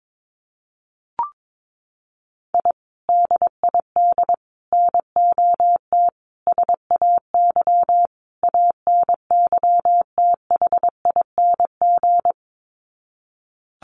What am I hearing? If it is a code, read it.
Morse code, heard as a keyed tone, about 22 wpm, 700 Hz. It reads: I DID NOT SAY ANYTHING
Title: Sets of 5 Words Encore2 22wpm